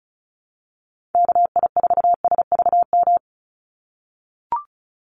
Morse code, heard as a keyed tone, about 34 wpm, 700 Hz. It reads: KI4SVM
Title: NA SOTA Activator Callsigns 35wpm